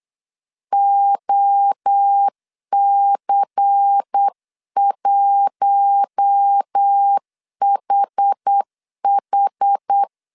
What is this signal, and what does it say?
Morse code, heard as a keyed tone, about 8 wpm, 800 Hz. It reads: OC1HH
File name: 8wpm